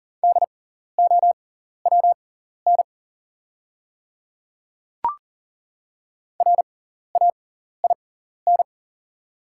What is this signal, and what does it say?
Morse code, heard as a keyed tone, about 40 wpm, spaced as for 14 wpm, 690 Hz. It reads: DOWN RAIN
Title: 6x Character Spacing - Top 300 Words - 40wpm